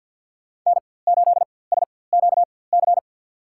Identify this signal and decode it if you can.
Morse code, heard as a keyed tone, about 50 wpm, spaced as for 23 wpm, 690 Hz. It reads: N8SQC
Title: USA Call Signs - 5 Characters - 4x Character Spacing - 50wpm